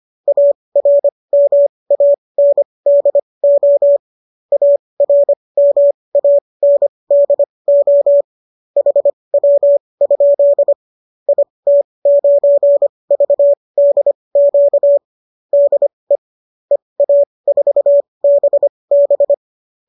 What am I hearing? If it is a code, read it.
Morse code, heard as a keyed tone, about 25 wpm, spaced as for 20 wpm, 570 Hz. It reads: ARMANDO ARMANDO HW? IT9VDQ DE EA4BB